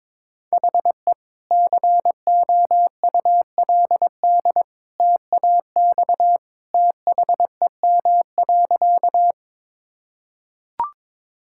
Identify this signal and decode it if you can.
Morse code, heard as a keyed tone, about 22 wpm, 700 Hz. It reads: HE COULD TAX THEM.
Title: Sentences from Top 1000 Words - Review Part 5 - 22wpm